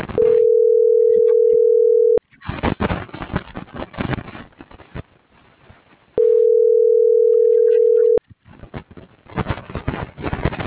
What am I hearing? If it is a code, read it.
Ringback tone (North America), 440 + 480 Hz.